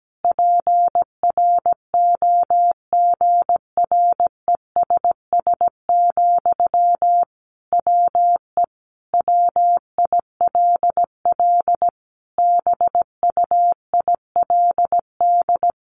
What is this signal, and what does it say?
Morse code, heard as a keyed tone, about 17 wpm, 700 Hz. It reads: PROGRESS, WE WILL BUILD
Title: Sentences from Top 1400 Words 17wpm